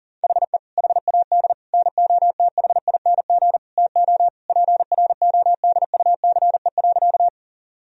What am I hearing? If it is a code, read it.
Morse code, heard as a keyed tone, about 40 wpm, 700 Hz. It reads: HE HAD NOTHING TO PRODUCE.
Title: Sentences from Top 1000 Words - Review Part 3 - 40wpm